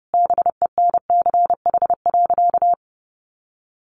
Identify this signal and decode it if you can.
Morse code, heard as a keyed tone, about 30 wpm, 700 Hz. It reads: BENCH.
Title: Sentences from Top 1400 Words 30wpm